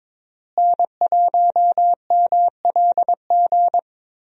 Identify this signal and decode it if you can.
Morse code, heard as a keyed tone, about 21 wpm, 700 Hz. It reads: N1MLG
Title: USA Call Signs - 5 Characters - Repeated 2x - 22wpm